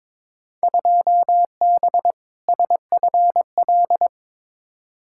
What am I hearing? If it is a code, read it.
Morse code, heard as a keyed tone, about 22 wpm, 700 Hz. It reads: 2B SFL
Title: Field Day Exchanges - 22wpm